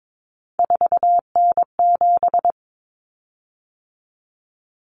Morse code, heard as a keyed tone, about 22 wpm, 700 Hz. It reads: 4N7
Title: Common Cut Numbers - 3-Digit 22wpm